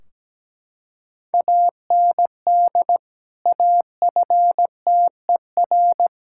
Morse code, heard as a keyed tone, about 17 wpm, 700 Hz. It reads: AND AFTER
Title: Sentences from Top 1600 Words 17wpm